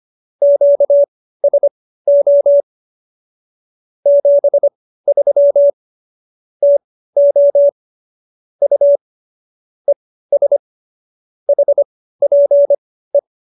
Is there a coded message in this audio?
Yes: Morse code from 0.4 to 13.2 s, heard as a keyed tone, about 25 wpm, spaced as for 15 wpm, 570 Hz. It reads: QSO 73 TO U ES HPE